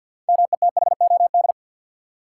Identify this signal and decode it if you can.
Morse code, heard as a keyed tone, about 48 wpm, 700 Hz. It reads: METHOD
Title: Six Letter Words - 50wpm